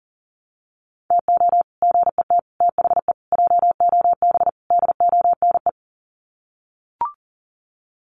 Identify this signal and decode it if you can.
Morse code, heard as a keyed tone, about 40 wpm, 700 Hz. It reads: TO GET THE JOB DONE
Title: Sets of 5 Words Encore2 40wpm